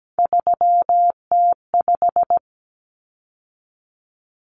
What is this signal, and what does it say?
Morse code, heard as a keyed tone, about 17 wpm, 700 Hz. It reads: 3T5
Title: Common Cut Numbers - 3-Digit 17wpm